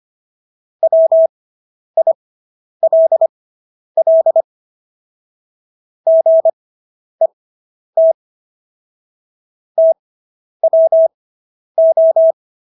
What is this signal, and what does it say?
Morse code, heard as a keyed tone, about 25 wpm, spaced as for 10 wpm, 650 Hz. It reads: WILL GET TWO